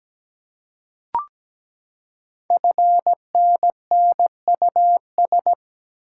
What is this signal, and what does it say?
Morse code, heard as a keyed tone, about 17 wpm, 700 Hz. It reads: FNNUS
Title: Code Groups - 5 Letters - 17wpm